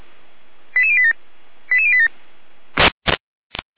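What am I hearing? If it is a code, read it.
Five-tone selective call (ZVEI), heard as quick runs of five tones, tones of about 70 ms. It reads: 88087 88087
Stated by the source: ZVEI five-tone selective call received by SDR.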